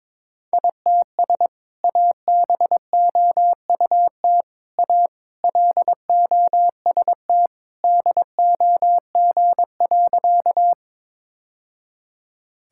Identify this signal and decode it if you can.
Morse code, heard as a keyed tone, about 22 wpm, 700 Hz. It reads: ITS ABOUT A LOST DOG.
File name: practice / 22wpm / Sentences with 2-Word Phrases - 22wpm